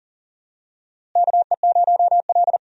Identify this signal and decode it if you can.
Morse code, heard as a keyed tone, about 40 wpm, 700 Hz. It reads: KE0L
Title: USA Call Signs - 4 Characters - Repeated 3x - 40wpm